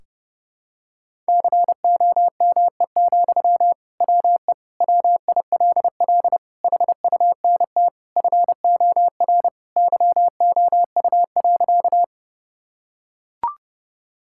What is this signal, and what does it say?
Morse code, heard as a keyed tone, about 30 wpm, 700 Hz. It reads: COME, WE WILL HUNT FOR YOU.
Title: Sentences from Top 1600 Words 30wpm